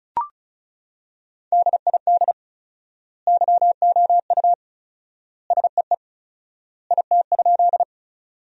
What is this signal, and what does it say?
Morse code, heard as a keyed tone, about 34 wpm, 700 Hz. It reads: DID YOU SEE IT?
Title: 4x Word-Spacing - Sentences from Top 100 Words - 35wpm